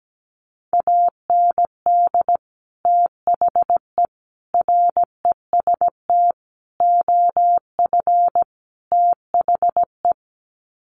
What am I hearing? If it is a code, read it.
Morse code, heard as a keyed tone, about 17 wpm, 700 Hz. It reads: AND THE REST OF THE
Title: Sets of 5 Words 17wpm